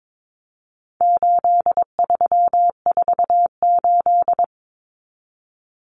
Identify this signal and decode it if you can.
Morse code, heard as a keyed tone, about 22 wpm, 700 Hz. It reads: 8348
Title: Common Cut Numbers - 4-Digit 22wpm